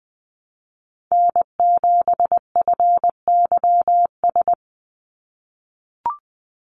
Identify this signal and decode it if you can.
Morse code, heard as a keyed tone, about 20 wpm, 700 Hz. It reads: N7FYS
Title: USA Call Signs 20wpm